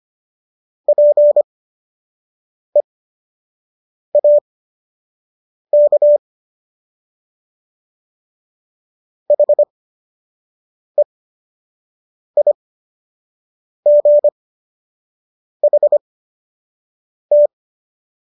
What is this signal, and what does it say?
Morse code, heard as a keyed tone, about 25 wpm, spaced as for 6 wpm, 590 Hz. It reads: PEAK HEIGHT